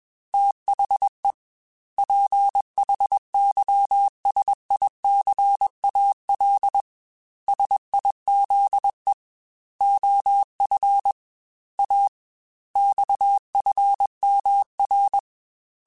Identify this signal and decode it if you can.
Morse code, heard as a keyed tone, about 21 wpm, 800 Hz. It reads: THE PHYSICAL SIZE OF A XFMR
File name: Technical Subjects / Technical Subjects-mp3s / Sw Pwr Sup @20